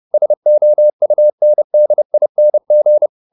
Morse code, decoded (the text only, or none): SOUNDING